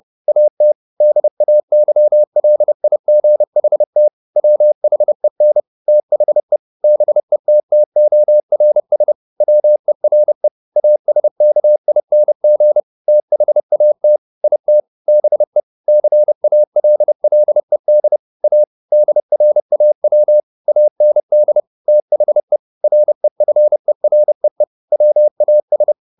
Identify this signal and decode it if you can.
Morse code, heard as a keyed tone, about 30 wpm, 600 Hz. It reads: AT DAYLIGHT WHEN THE BETTORS WERE ASKING THAT IT BE CALLED A DRAW AND THE REFEREE WAS